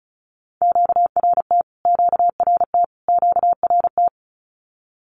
Morse code, heard as a keyed tone, about 34 wpm, 700 Hz. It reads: QRT QRT QRT
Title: Q-Codes Repeated 3x 35wpm